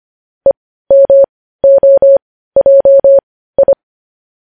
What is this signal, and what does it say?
Morse code, heard as a keyed tone, about 25 wpm, spaced as for 15 wpm, 550 Hz. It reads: EMOJI
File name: podcast / News headlines-1x-25x15